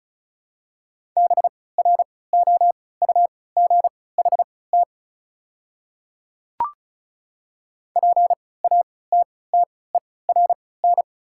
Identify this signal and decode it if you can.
Morse code, heard as a keyed tone, about 36 wpm, spaced as for 20 wpm, 700 Hz. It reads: BROUGHT PATTERN
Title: 3x Character Spacing - Top 500 Words - 35wpm